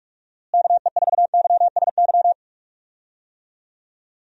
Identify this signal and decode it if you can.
Morse code, heard as a keyed tone, about 41 wpm, 700 Hz. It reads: KE4YSY
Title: USA Call Signs - 6 Characters - Repeated 2x - 45wpm